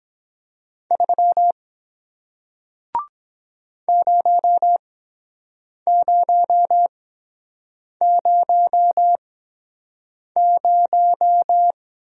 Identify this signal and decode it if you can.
Morse code, heard as a keyed tone, about 23 wpm, spaced as for 7 wpm, 700 Hz. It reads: 3 0000